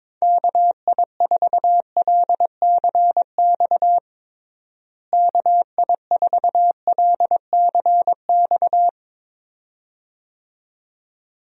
Morse code, heard as a keyed tone, about 22 wpm, 700 Hz. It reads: KI4LCX KI4LCX
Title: USA Call Signs - 6 Characters - Repeated 2x - 22wpm